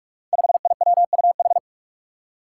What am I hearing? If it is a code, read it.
Morse code, heard as a keyed tone, about 44 wpm, 700 Hz. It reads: 5IWUH
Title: Five Challenging Letter-Number Combinations 45wpm